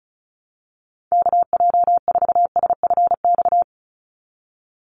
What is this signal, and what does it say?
Morse code, heard as a keyed tone, about 34 wpm, 700 Hz. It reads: KJ4SFX
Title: USA Call Signs 35wpm